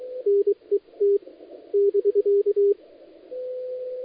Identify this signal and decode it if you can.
Morse code, heard as a keyed tone, about 23 wpm, spaced as for 20 wpm, 400 Hz. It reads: NET <BK>
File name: dlazaawansowanych / nagraniazpasma / titawaNETcontrol